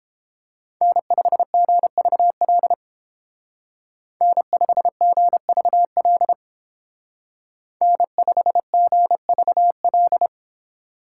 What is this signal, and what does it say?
Morse code, heard as a keyed tone, about 28 wpm, 700 Hz. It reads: N5GVL N5GVL N5GVL